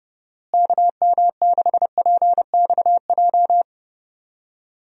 Morse code, heard as a keyed tone, about 30 wpm, 700 Hz. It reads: KM6PXJ
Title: USA Call Signs - 6 Characters - Repeated 3x - 30wpm